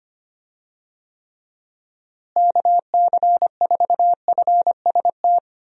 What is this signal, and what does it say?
Morse code, heard as a keyed tone, about 25 wpm, 700 Hz. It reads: KC4FST
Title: USA Call Signs - 6 Characters - Repeated 2x - 25wpm